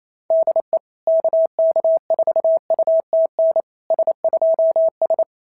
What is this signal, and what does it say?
Morse code, heard as a keyed tone, about 28 wpm, 690 Hz. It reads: DE KK4UTN S2S
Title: SOTA QSO Practice 28wpm